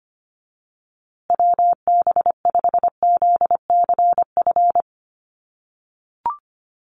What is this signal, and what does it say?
Morse code, heard as a keyed tone, about 25 wpm, 700 Hz. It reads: WB5ZCF
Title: USA Call Signs Repeated 3x 25wpm